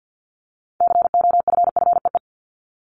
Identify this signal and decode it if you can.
Morse code, heard as a keyed tone, about 50 wpm, 700 Hz. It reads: COFFEE
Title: Top 1751-2000 Words 50wpm